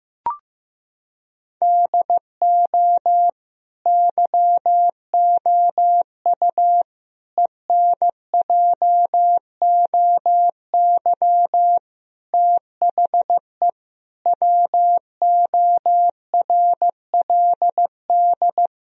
Morse code, heard as a keyed tone, about 15 wpm, 700 Hz. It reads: DO YOU ENJOY THE WORLD
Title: Sentences from Top 1000 Words - Review Part 4 - 15wpm